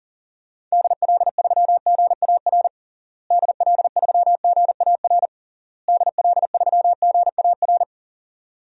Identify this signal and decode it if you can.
Morse code, heard as a keyed tone, about 40 wpm, 700 Hz. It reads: DL3GAR DL3GAR DL3GAR
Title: DXCC-100 Call Signs - 6 Characters - Repeated 3x - 40wpm